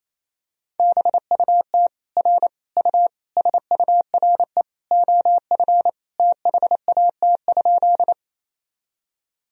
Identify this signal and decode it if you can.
Morse code, heard as a keyed tone, about 28 wpm, 700 Hz. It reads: BUT R U SURE OF THAT?
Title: Sentences from Top 300 Words and Abbreviations 28wpm